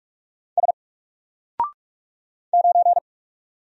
Morse code, heard as a keyed tone, about 44 wpm, 700 Hz. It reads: S 9
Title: Single Character (Letter:Number:Punctuation:Prosign) 45wpm